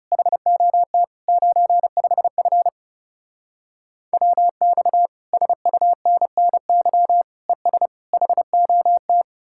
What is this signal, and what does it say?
Morse code, heard as a keyed tone, about 38 wpm, 700 Hz. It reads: HOT 95F WX SUNNY ES HOT